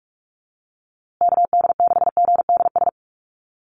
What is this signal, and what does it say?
Morse code, heard as a keyed tone, about 44 wpm, 700 Hz. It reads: KD6GDS
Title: USA Call Signs Repeated 3x 45wpm